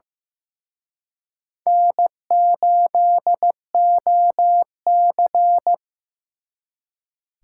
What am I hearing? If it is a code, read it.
Morse code, heard as a keyed tone, about 17 wpm, 700 Hz. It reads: N8OC
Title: Speed Racing - USA 1x2 Call Signs 15wpm